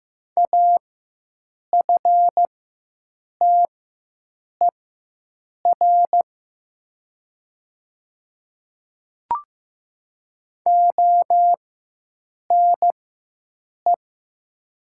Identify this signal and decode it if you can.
Morse code, heard as a keyed tone, about 15 wpm, spaced as for 7 wpm, 700 Hz. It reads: AFTER ONE
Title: 4x Character Spacing - Top 200 Words - 15wpm